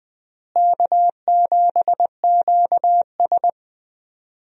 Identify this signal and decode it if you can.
Morse code, heard as a keyed tone, about 20 wpm, 700 Hz. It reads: K7QS
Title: USA Call Signs - 4 Characters - Repeated 3x - 20wpm